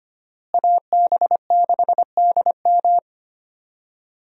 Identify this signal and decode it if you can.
Morse code, heard as a keyed tone, about 25 wpm, 700 Hz. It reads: AB6DM
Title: USA Call Signs - 5 Characters - Repeated 2x - 25wpm